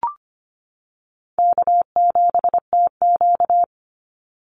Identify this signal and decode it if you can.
Morse code, heard as a keyed tone, about 25 wpm, 700 Hz. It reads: K7TQ
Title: NA SOTA Chaser Callsigns 25wpm